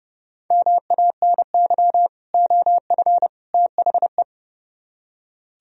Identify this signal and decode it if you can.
Morse code, heard as a keyed tone, about 39 wpm, 700 Hz. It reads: MANY OF THE